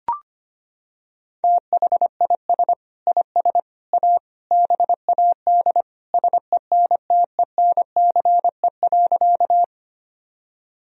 Morse code, heard as a keyed tone, about 25 wpm, 700 Hz. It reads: THIS IS A BAD SENTENCE.
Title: Sentences from Top 1000 Words - Review Part 1 - 25wpm